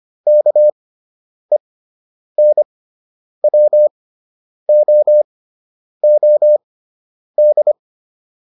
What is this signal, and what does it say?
Morse code, heard as a keyed tone, about 25 wpm, spaced as for 9 wpm, 600 Hz. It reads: KENWOOD